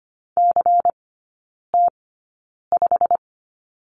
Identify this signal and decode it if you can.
Morse code, heard as a keyed tone, about 25 wpm, spaced as for 9 wpm, 700 Hz. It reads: CT5
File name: course / 25:10wpm / 093 Lesson 16e - Letter c - Three Characters 25wpm